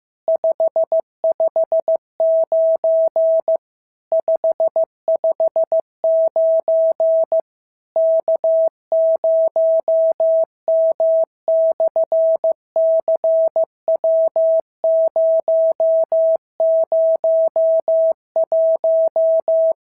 Morse code, heard as a keyed tone, about 14 wpm, 650 Hz. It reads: 559 559 K0M/CW001
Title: SOTA QSO Practice 15wpm